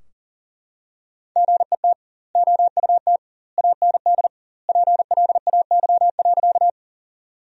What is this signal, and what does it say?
Morse code, heard as a keyed tone, about 40 wpm, 700 Hz. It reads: GET OUT AND PLAY.
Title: 2x Word-Spacing - Sentences from Top 200 Words - 40wpm